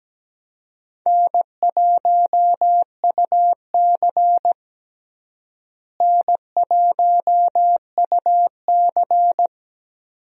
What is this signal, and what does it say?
Morse code, heard as a keyed tone, about 17 wpm, 700 Hz. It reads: N1UC N1UC
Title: USA Call Signs - 4 Characters - Repeated 2x - 17wpm